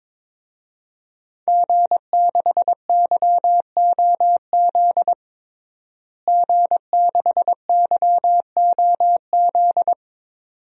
Morse code, heard as a keyed tone, about 22 wpm, 700 Hz. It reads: G6YOZ G6YOZ
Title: DXCC-100 Call Signs - 5 Characters - Repeated 2x - 22wpm